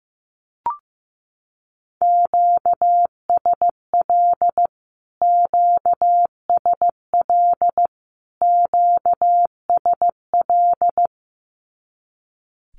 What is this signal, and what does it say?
Morse code, heard as a keyed tone, about 15 wpm, 700 Hz. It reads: QSL QSL QSL
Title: Q-Codes Repeated 3x 15wpm